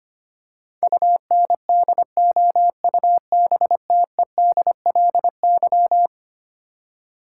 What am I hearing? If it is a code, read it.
Morse code, heard as a keyed tone, about 25 wpm, 700 Hz. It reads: UNDOUBTEDLY
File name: practice / 25wpm / Eleven Letter Words - 25wpm